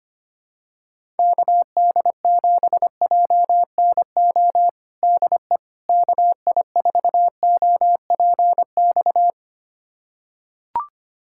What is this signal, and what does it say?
Morse code, heard as a keyed tone, about 25 wpm, 700 Hz. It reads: KD7JNO DE KI4OPX
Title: USA Callsign DE Callsign 25wpm